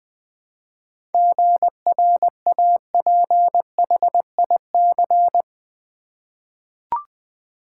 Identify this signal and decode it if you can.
Morse code, heard as a keyed tone, about 20 wpm, 700 Hz. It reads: GRAPHIC
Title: 7 Letter Words - First 3 Letters Given - 20wpm